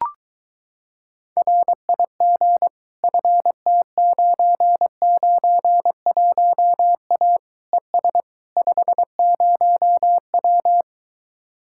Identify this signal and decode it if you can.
Morse code, heard as a keyed tone, about 23 wpm, 700 Hz. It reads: RIG FT991A ES 50W